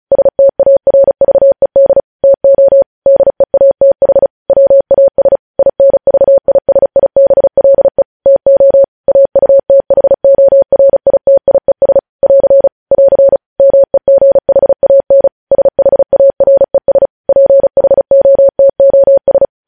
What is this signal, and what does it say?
Morse code, heard as a keyed tone, about 36 wpm, 550 Hz. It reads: STARVED TO DEATH WAS INVISIBLE TO AUTHORITIES <AR> <AR> MEGHAN SHARES PHOTOS
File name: podcast / News headlines-1x-35x35